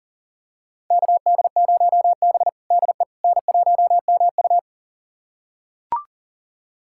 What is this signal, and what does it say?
Morse code, heard as a keyed tone, about 40 wpm, 700 Hz. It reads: KD0B DE N1MU
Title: USA Callsign DE Callsign - 4 Characters - 40wpm